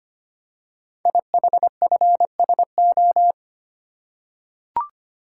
Morse code, heard as a keyed tone, about 25 wpm, 700 Hz. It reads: IHFSO
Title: Code Groups - 5 Letters - 25wpm